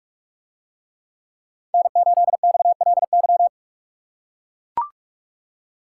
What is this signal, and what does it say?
Morse code, heard as a keyed tone, about 42 wpm, 700 Hz. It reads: N8XLY